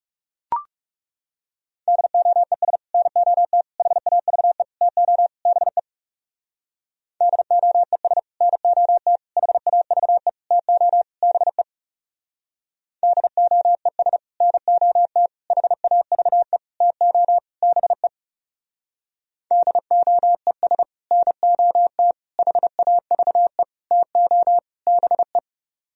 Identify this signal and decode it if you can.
Morse code, heard as a keyed tone, about 38 wpm, 700 Hz. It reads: DOES NOT HAVE TO BE DOES NOT HAVE TO BE DOES NOT HAVE TO BE DOES NOT HAVE TO BE
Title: Speed-Racing - Sets of 5 Words 30wpm